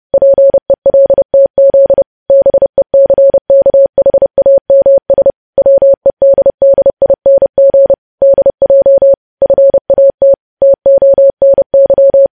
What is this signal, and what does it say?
Morse code, heard as a keyed tone, about 30 wpm, 550 Hz. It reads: PELTZ BECKHAMS WEDDING DJ FAT TONY